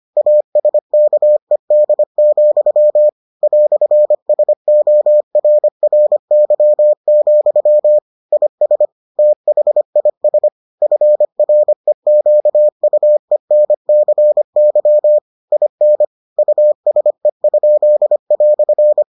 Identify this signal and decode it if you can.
Morse code, heard as a keyed tone, about 25 wpm, 600 Hz. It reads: ASKED, "SORRY, IS THIS FREQUENCY IN USE?"